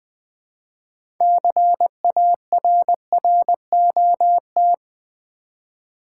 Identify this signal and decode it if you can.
Morse code, heard as a keyed tone, about 20 wpm, 700 Hz. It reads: CARROT
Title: Easy Words with Hints - 20wpm